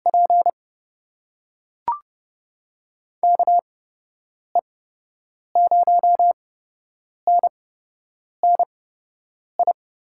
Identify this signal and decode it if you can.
Morse code, heard as a keyed tone, about 31 wpm, spaced as for 8 wpm, 700 Hz. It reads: P KE0NNI
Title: USA Call Signs - 8x Character Spacing - 30wpm